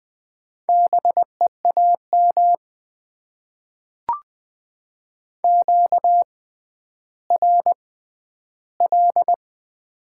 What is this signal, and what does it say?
Morse code, heard as a keyed tone, about 20 wpm, spaced as for 11 wpm, 700 Hz. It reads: BEAM QRL
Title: Top 90 QSO Elements - 6x Character Spacing - 20wpm